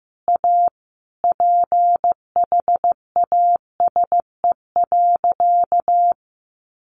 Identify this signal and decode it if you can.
Morse code, heard as a keyed tone, about 15 wpm, 700 Hz. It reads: A PHASE.
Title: Sentences from Top 1200 Words 15wpm